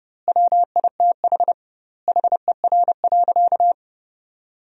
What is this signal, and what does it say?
Morse code, heard as a keyed tone, about 30 wpm, 700 Hz. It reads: WITH HER.
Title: 2x Word-Spacing - Sentences from Top 100 Words - 30wpm